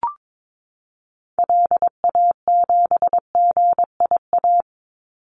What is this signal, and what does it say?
Morse code, heard as a keyed tone, about 22 wpm, 700 Hz. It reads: LA7GIA
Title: Call Signs Repeated 2x - 22wpm